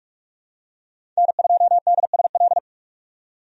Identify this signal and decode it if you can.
Morse code, heard as a keyed tone, about 45 wpm, 700 Hz. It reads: N2DSL
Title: USA Call Signs - 5 Characters - 45wpm